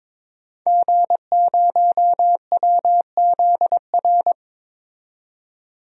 Morse code, heard as a keyed tone, about 22 wpm, 700 Hz. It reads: G0WZR
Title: DXCC-100 Call Signs - 5 Characters - 22wpm